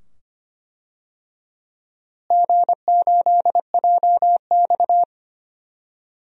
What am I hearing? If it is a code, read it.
Morse code, heard as a keyed tone, about 25 wpm, 700 Hz. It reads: G8JX
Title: DXCC-100 Call Signs - 4 Characters - Repeated 2x - 25wpm